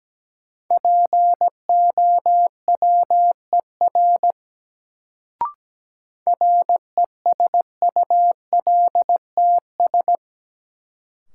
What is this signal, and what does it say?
Morse code, heard as a keyed tone, about 17 wpm, 700 Hz. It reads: POWER RESULTS
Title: Sets of 4 Words - New Word Drill - 17wpm